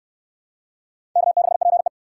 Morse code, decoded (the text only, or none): K6PE